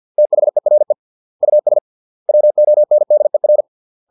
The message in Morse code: THERE US WONDER

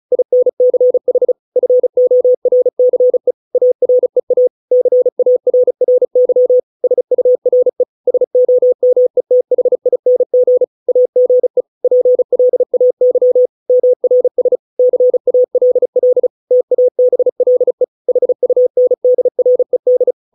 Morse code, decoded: INCH FORCE AREA CARRY SURE SOMETHING AGE PLAY MRS CALL TABLE HUNDRED

